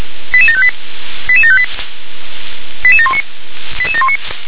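Five-tone selective call (ZVEI), heard as quick runs of five tones, tones of about 70 ms. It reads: 88758 88758 88719 88719
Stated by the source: ZVEI five-tone selective call received on an SDR.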